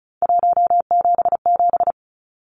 Morse code, heard as a keyed tone, about 34 wpm, 700 Hz. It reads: 177
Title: Numbers - 3-Digit 35wpm